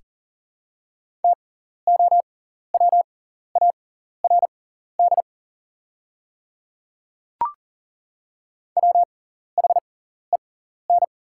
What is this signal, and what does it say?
Morse code, heard as a keyed tone, about 40 wpm, spaced as for 14 wpm, 700 Hz. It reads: TOWARD WHEN
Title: 6x Character Spacing - Top 500 Words - 40wpm